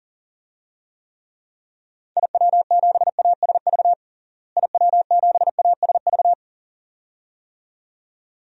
Morse code, heard as a keyed tone, about 40 wpm, 700 Hz. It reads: IW7ASV IW7ASV
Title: DXCC-100 Call Signs - 6 Characters - Repeated 2x - 40wpm